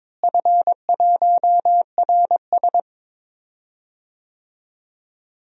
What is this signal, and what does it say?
Morse code, heard as a keyed tone, about 22 wpm, 700 Hz. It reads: F1RS
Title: DXCC-100 Call Signs - 4 Characters - Repeated 2x - 22wpm